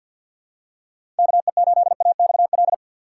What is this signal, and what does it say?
Morse code, heard as a keyed tone, about 50 wpm, 700 Hz. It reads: KE8AXL
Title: USA Call Signs - 6 Characters - 6x Character Spacing - 50wpm